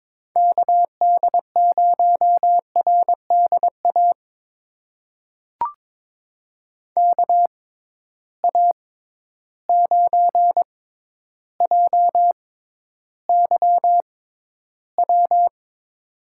Morse code, heard as a keyed tone, about 21 wpm, spaced as for 11 wpm, 700 Hz. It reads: KD0RDA KA9JYW
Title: USA Call Signs - 6 Characters - 6x Character Spacing - 22wpm